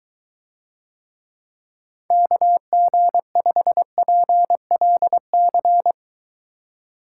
Morse code, heard as a keyed tone, about 23 wpm, 700 Hz. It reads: KG5PLC